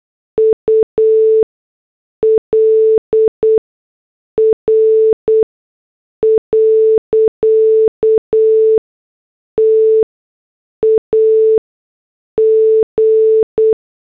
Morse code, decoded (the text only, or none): UL R. TAG